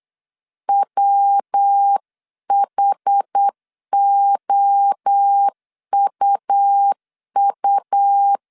Morse code, heard as a keyed tone, about 8 wpm, 800 Hz. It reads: WHOUU